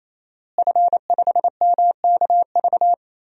Morse code, heard as a keyed tone, about 28 wpm, 700 Hz. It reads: F5MKV
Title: DXCC-100 Call Signs - 5 Characters - Repeated 3x - 28wpm